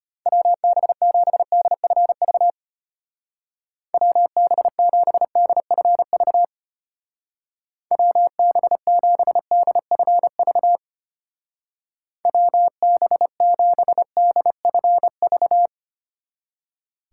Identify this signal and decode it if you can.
Morse code, heard as a keyed tone, about 31 wpm, 700 Hz. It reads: WB7DFV WB7DFV WB7DFV WB7DFV
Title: Speed Racing - USA Call Signs 25wpm